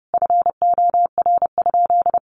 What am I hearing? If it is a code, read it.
Morse code, heard as a keyed tone, about 28 wpm, 700 Hz. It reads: FOR?
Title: Sentences from Top 400 Words 30wpm